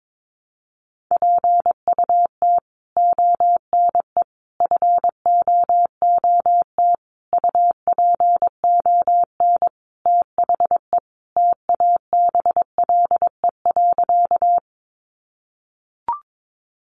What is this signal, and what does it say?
Morse code, heard as a keyed tone, about 22 wpm, 700 Hz. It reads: PUT ONE FOOT UPON THE TABLE.
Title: Sentences from Top 1100 Words 22wpm